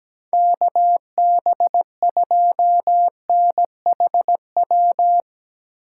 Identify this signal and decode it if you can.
Morse code, heard as a keyed tone, about 17 wpm, 700 Hz. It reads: KB2NHW
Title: USA Call Signs - 8x Character Spacing - 17wpm